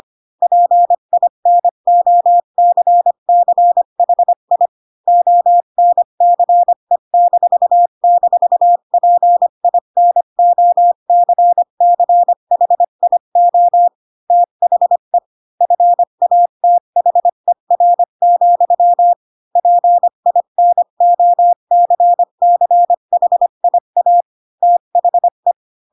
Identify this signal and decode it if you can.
Morse code, heard as a keyed tone, about 25 wpm, 700 Hz. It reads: PINOCCHI ONCE--PINOCCHIO THE FATHER, PINOCCHIA THE
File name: Pinocchio / Pinocchio ch3 20 25 ik1hge